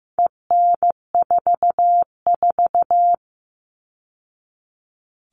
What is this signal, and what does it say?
Morse code, heard as a keyed tone, about 15 wpm, 700 Hz. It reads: EN44
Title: All Cut Numbers - 4-Digit 15wpm